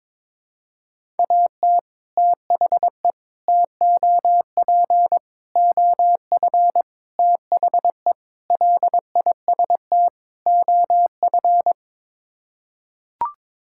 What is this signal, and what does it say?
Morse code, heard as a keyed tone, about 22 wpm, 700 Hz. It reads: AT THE TOP OF THE LIST OF
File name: practice / 22wpm / Sets of 7 Words - 22wpm